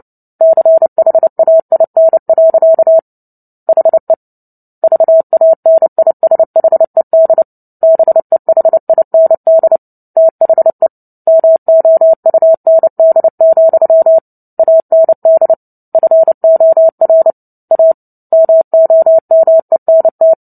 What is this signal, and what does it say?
Morse code, heard as a keyed tone, about 30 wpm, 650 Hz. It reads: CHAIN. HE VANISHED BEHIND THE MOUND, AND FOR A MOMENT
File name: Book Two / 03-Chapter 3 - 29 WPM